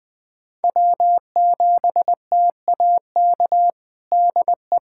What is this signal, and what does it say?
Morse code, heard as a keyed tone, about 20 wpm, 700 Hz. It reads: W7TAK DE
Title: USA Callsign DE Callsign - 5 Characters - 20wpm